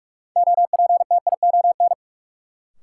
Morse code, heard as a keyed tone, about 44 wpm, 700 Hz. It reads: OPTION
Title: Sentences from Top 1300 Words - New Word Drill - 45wpm